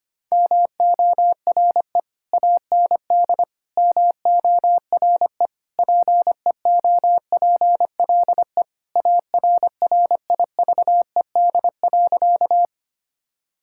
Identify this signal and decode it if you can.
Morse code, heard as a keyed tone, about 25 wpm, 700 Hz. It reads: MORE AND MORE PEOPLE ARRIVED.
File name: practice / 25wpm / Sentences with 3-Word Phrases - 25wpm